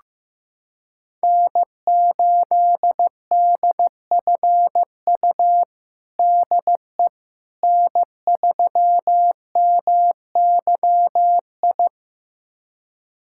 Morse code, heard as a keyed tone, about 15 wpm, 700 Hz. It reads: N8DFU DE N3MYI
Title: USA Callsign DE Callsign - 5 Characters - 15wpm